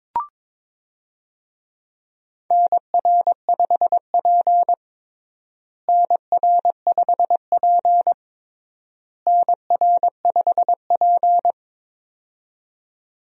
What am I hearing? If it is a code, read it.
Morse code, heard as a keyed tone, about 23 wpm, 700 Hz. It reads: NR5P NR5P NR5P
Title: USA Call Signs - 4 Characters - Repeated 3x - 22wpm